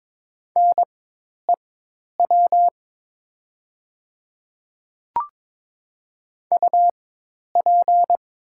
Morse code, heard as a keyed tone, about 22 wpm, spaced as for 10 wpm, 700 Hz. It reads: NEW UP
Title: 4x Character Spacing - Top 500 Words - 22wpm